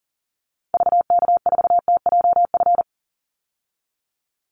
Morse code, heard as a keyed tone, about 40 wpm, 700 Hz. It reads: VK4TJF
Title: Call Signs Repeated 3x - 40wpm